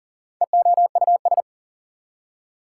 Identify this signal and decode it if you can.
Morse code, heard as a keyed tone, about 40 wpm, 700 Hz. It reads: EOUS
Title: Prefixes and Suffixes 40wpm